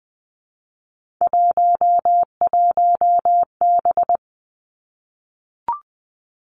Morse code, heard as a keyed tone, about 20 wpm, 700 Hz. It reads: 11B
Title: Three Challenging Letter-Number Combinations 20wpm